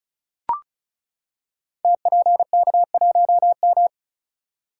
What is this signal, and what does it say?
Morse code, heard as a keyed tone, about 36 wpm, 700 Hz. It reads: TPK1M
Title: Code Groups - 5 Letter:Numbers - 35wpm